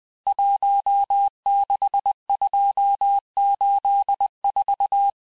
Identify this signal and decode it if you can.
Morse code, heard as a keyed tone, about 20 wpm, 800 Hz. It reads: 16284
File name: cours-858-texte-chiffre-vitesse-19mpm